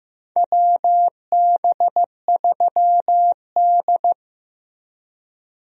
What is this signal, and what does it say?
Morse code, heard as a keyed tone, about 15 wpm, 700 Hz. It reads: WB3D
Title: USA Call Signs - 4 Characters - Repeated 3x - 15wpm